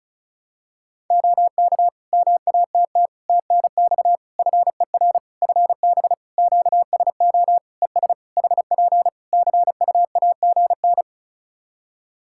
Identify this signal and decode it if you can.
Morse code, heard as a keyed tone, about 34 wpm, 700 Hz. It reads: OK MATT TNX FER FB QSO ES HP CUAGN
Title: Practice QSO - Exchange 4 - Ending - 35wpm